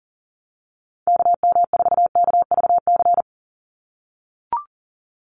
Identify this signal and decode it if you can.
Morse code, heard as a keyed tone, about 40 wpm, 700 Hz. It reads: KM4KVC
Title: USA Call Signs 40wpm